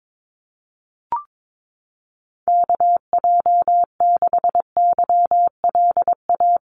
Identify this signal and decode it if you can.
Morse code, heard as a keyed tone, about 22 wpm, 700 Hz. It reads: KJ6YLA
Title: USA Call Signs Repeated 2x 22wpm